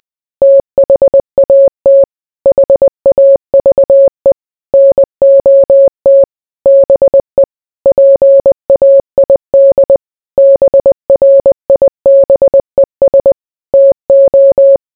Morse code, heard as a keyed tone, about 20 wpm, 550 Hz. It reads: THAT HAVE NOT BE PAID BRIBES TO